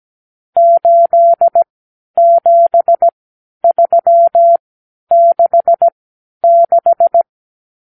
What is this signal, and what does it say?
Morse code, heard as a keyed tone, about 17 wpm, spaced as for 10 wpm, 680 Hz. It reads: 87366
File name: k7qo / Lesson 54 (9)